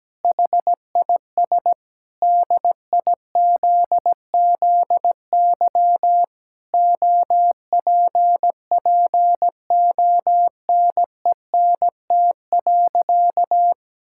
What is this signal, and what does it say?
Morse code, heard as a keyed tone, about 17 wpm, 700 Hz. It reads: HIS DIZZY OPPONENT.
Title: Pangram Challenge v2 - 17wpm